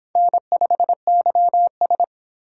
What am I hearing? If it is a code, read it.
Morse code, heard as a keyed tone, about 25 wpm, 700 Hz. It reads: N5YS